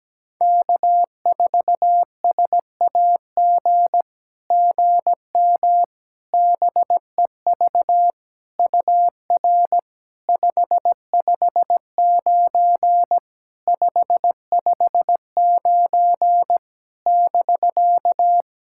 Morse code, heard as a keyed tone, about 17 wpm, 700 Hz. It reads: K4SAG GM BEV UR 559 559 <BK>
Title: SOTA QSO Practice 17wpm